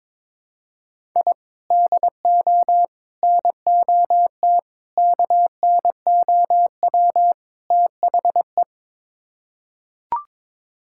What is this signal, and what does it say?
Morse code, heard as a keyed tone, about 21 wpm, 700 Hz. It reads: I DO NOT KNOW THE
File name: practice / 22wpm / Build A Phrase - 5-Word Phrases - 22wpm